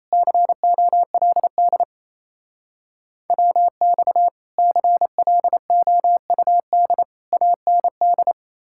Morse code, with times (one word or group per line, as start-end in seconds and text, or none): 0.1-1.8 COLD
3.3-4.3 WX
4.6-7.0 CLOUD
7.3-8.3 AND